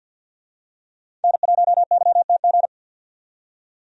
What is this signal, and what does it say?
Morse code, heard as a keyed tone, about 51 wpm, 700 Hz. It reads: N1YTG
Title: USA Call Signs - 5 Characters - 50wpm